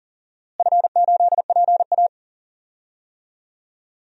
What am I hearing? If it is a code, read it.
Morse code, heard as a keyed tone, about 40 wpm, 700 Hz. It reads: F8PA
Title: DXCC-100 Call Signs - 4 Characters - Repeated 2x - 40wpm